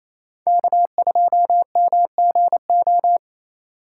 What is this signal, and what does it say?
Morse code, heard as a keyed tone, about 28 wpm, 700 Hz. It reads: K2MGO
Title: USA Call Signs - 3x Character Spacing - 28wpm